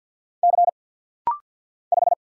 Morse code, heard as a keyed tone, about 50 wpm, 700 Hz. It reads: C 5
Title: Single Letter-Number Alt - Rapid-Fire 50wpm